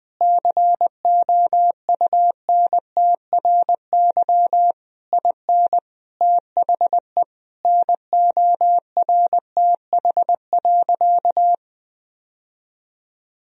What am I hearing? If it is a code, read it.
Morse code, heard as a keyed tone, about 20 wpm, 700 Hz. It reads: COUNTRY IN THE NORTH.
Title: Sentences from Top 1000 Words - Review Part 1 - 20wpm